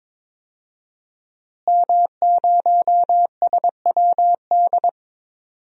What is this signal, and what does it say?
Morse code, heard as a keyed tone, about 22 wpm, 700 Hz. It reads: M0SWD
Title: DXCC-100 Call Signs - 5 Characters - Repeated 2x - 22wpm